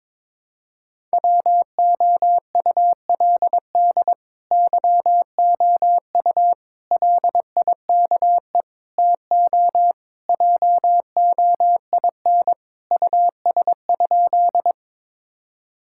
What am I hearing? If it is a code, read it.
Morse code, heard as a keyed tone, about 22 wpm, 700 Hz. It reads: WOULD YOU LIKE TO JOIN US?
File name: practice / 22wpm / Sentences with 4-Word Phrases - 22wpm